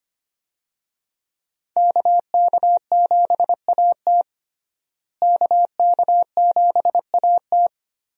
Morse code, heard as a keyed tone, about 25 wpm, 700 Hz. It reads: KK7AT KK7AT
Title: USA Call Signs - 5 Characters - Repeated 3x - 25wpm